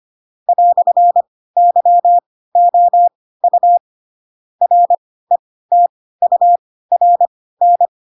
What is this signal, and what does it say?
Morse code, heard as a keyed tone, about 25 wpm, spaced as for 16 wpm, 700 Hz. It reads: "YOU RETURN
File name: Pinocchio / Pinocchio ch2 10 25 ik1hge